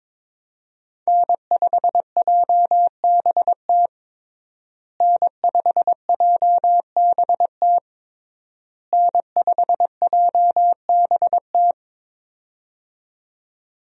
Morse code, heard as a keyed tone, about 22 wpm, 700 Hz. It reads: N5JBT N5JBT N5JBT
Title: USA Call Signs - 5 Characters - Repeated 3x - 22wpm